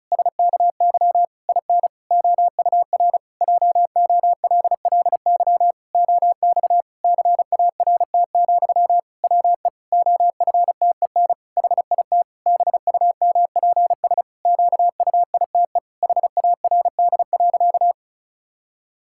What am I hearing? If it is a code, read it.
Morse code, heard as a keyed tone, about 36 wpm, 700 Hz. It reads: SKY IN OUR JOLLY OX CART, WE OFTEN HIT BUMPS QUITE HARD.
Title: Pangram Challenge v2 - 35wpm